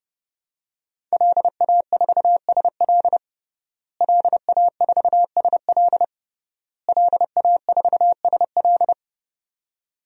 Morse code, heard as a keyed tone, about 30 wpm, 700 Hz. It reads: LA4SL LA4SL LA4SL
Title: DXCC-100 Call Signs - 5 Characters - Repeated 3x - 30wpm